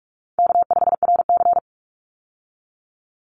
Morse code, heard as a keyed tone, about 44 wpm, 700 Hz. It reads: K5RC
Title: Call Signs - 45wpm